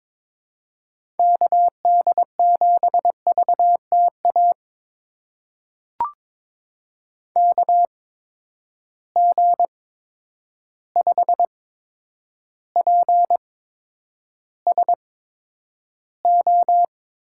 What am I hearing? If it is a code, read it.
Morse code, heard as a keyed tone, about 23 wpm, spaced as for 10 wpm, 700 Hz. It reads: KD7VTA KG5PSO
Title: USA Call Signs - 6 Characters - 8x Character Spacing - 22wpm